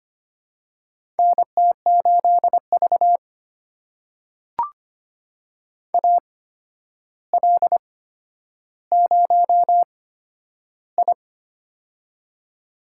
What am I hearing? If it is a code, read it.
Morse code, heard as a keyed tone, about 25 wpm, spaced as for 11 wpm, 700 Hz. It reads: NT8V AL0I
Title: USA Call Signs - 4 Characters - 8x Character Spacing - 25wpm